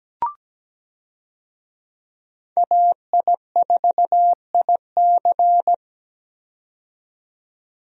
Morse code, heard as a keyed tone, about 17 wpm, 700 Hz. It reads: AI4IC
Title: USA Call Signs - 5 Characters - 17wpm